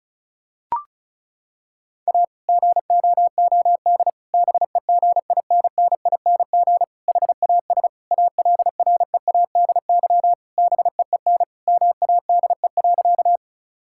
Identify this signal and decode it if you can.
Morse code, heard as a keyed tone, about 36 wpm, 690 Hz. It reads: A GOOD BEGINNING HAS ALREADY BEEN MADE.
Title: Sentences from Top 1000 Words - Review Part 5 - 35wpm